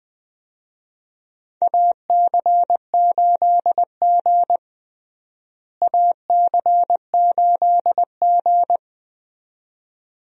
Morse code, heard as a keyed tone, about 21 wpm, 700 Hz. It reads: AC8G AC8G
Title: USA Call Signs - 4 Characters - Repeated 2x - 20wpm